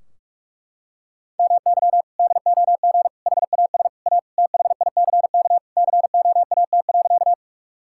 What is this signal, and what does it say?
Morse code, heard as a keyed tone, about 41 wpm, 700 Hz. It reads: MY DOG HAS A THICK COAT.
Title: Sentences from Top 1600 Words 45wpm